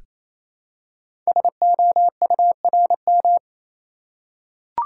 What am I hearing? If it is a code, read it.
Morse code, heard as a keyed tone, about 29 wpm, 700 Hz. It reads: SOURM
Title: Code Groups - 5 Letters - 28wpm